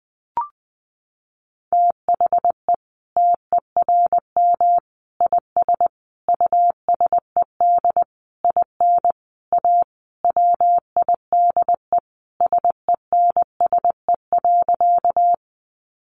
Morse code, heard as a keyed tone, about 20 wpm, 700 Hz. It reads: THE TERM IS USED IN A WIDE SENSE.
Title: Sentences from Top 900 Words 20wpm